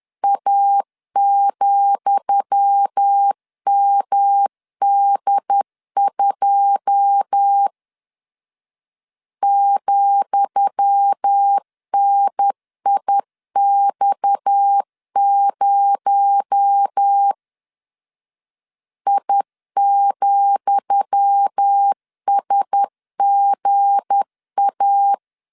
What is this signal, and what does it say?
Morse code, heard as a keyed tone, about 10 wpm, 800 Hz. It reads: A,MD2 ,NIX0 I,SGA
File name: LEZIONI / 10 SIGNS / 10D COMA 10wpm 3013